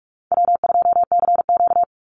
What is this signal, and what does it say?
Morse code, heard as a keyed tone, about 44 wpm, 700 Hz. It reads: W2CQ
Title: Easy Call Signs Repeated 2x - 45wpm